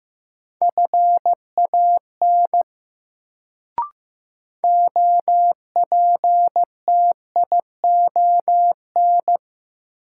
Morse code, heard as a keyed tone, about 15 wpm, 700 Hz. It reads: FAN OPTION
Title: Sentences from Top 1300 Words - New Word Drill Alt - 15wpm